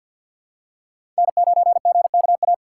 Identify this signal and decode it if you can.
Morse code, heard as a keyed tone, about 52 wpm, 700 Hz. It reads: N9GKA
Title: USA Call Signs - 5 Characters - 6x Character Spacing - 50wpm